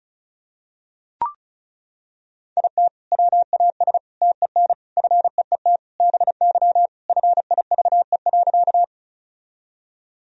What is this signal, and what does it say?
Morse code, heard as a keyed tone, about 36 wpm, 700 Hz. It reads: IT WAS TEN FEET BY FIVE.
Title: Sentences from Top 1000 Words - Review Part 2 - 35wpm